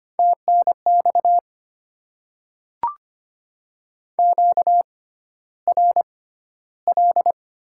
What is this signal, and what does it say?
Morse code, heard as a keyed tone, about 25 wpm, spaced as for 13 wpm, 700 Hz. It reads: TNX QRL
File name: Top 20 QSO Elements - 6x Character Spacing - 25wpm